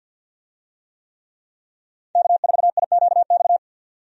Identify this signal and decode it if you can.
Morse code, heard as a keyed tone, about 52 wpm, 700 Hz. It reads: K4IQX
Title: USA Call Signs - 5 Characters - 50wpm